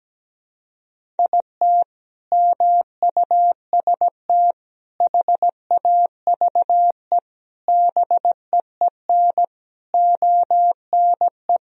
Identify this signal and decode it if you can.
Morse code, heard as a keyed tone, about 17 wpm, 700 Hz. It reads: IT MUST HAVE BEEN ONE
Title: Sentences from Top 1000 Words - Review Part 1 - 17wpm